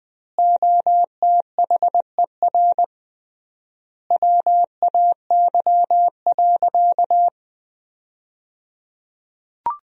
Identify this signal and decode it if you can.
Morse code, heard as a keyed tone, about 20 wpm, 700 Hz. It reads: OTHER WAY.
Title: 3x Word-Spacing - Sentences from Top 100 Words - 20wpm